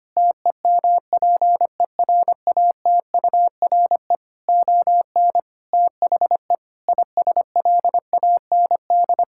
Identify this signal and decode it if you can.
Morse code, heard as a keyed tone, about 25 wpm, 700 Hz. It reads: TEMPERATURE ON THE ISLAND
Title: Sentences from Top 1000 Words - Review Part 6 - 25wpm